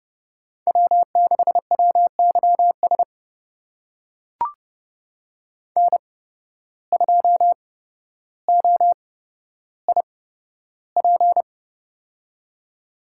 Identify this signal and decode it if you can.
Morse code, heard as a keyed tone, about 30 wpm, spaced as for 13 wpm, 700 Hz. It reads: W6WYS N2OIP
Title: USA Call Signs - 8x Character Spacing - 30wpm